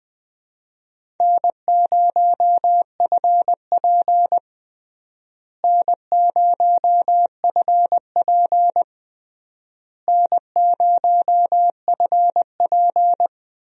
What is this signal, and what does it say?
Morse code, heard as a keyed tone, about 20 wpm, 700 Hz. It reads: N0FP N0FP N0FP
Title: USA Call Signs - 4 Characters - Repeated 3x - 20wpm